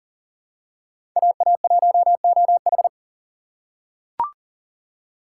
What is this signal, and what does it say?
Morse code, heard as a keyed tone, about 38 wpm, 700 Hz. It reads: AA1OH
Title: USA Call Signs - 2x Character Spacing - 40wpm